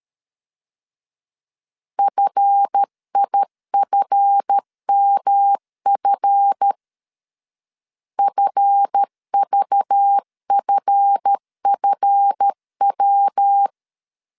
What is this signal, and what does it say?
Morse code, heard as a keyed tone, about 13 wpm, 800 Hz. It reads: FIFMF FVFFW
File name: LEZIONI / 7 GF9Z / 7 GF 12wpm 2122